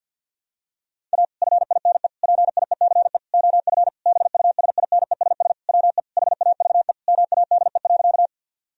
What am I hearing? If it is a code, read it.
Morse code, heard as a keyed tone, about 52 wpm, 700 Hz. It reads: A FINE PIECE OF BUSINESS WE HAVE MADE.
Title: Sentences from Top 1000 Words - Review Part 2 - 50wpm